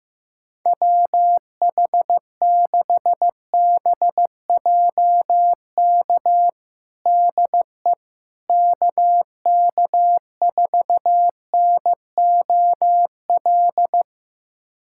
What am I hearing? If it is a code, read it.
Morse code, heard as a keyed tone, about 15 wpm, 700 Hz. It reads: WH6BJK DE KK4NOL
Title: USA Callsign DE Callsign 15wpm